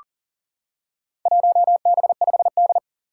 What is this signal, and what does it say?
Morse code, heard as a keyed tone, about 40 wpm, 700 Hz. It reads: 1B5D
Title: Four Challenging Letter-Number Combinations 40wpm